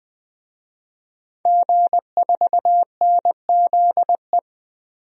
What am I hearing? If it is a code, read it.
Morse code, heard as a keyed tone, about 20 wpm, 700 Hz. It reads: G4NZE
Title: DXCC-100 Call Signs - 5 Characters - Repeated 2x - 20wpm